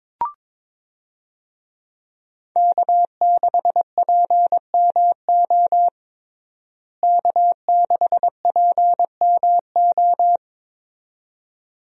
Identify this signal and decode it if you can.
Morse code, heard as a keyed tone, about 22 wpm, 700 Hz. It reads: K6PMO K6PMO
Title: USA Call Signs - 5 Characters - Repeated 2x - 22wpm